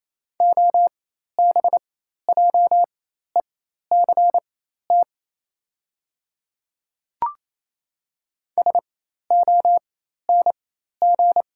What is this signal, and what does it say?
Morse code, heard as a keyed tone, about 28 wpm, spaced as for 13 wpm, 690 Hz. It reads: OBJECT SONG
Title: 4x Character Spacing - Top 500 Words - 28wpm